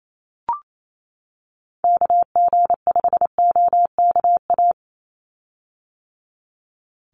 Morse code, heard as a keyed tone, about 28 wpm, 700 Hz. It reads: KG5OKA
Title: USA Call Signs 28wpm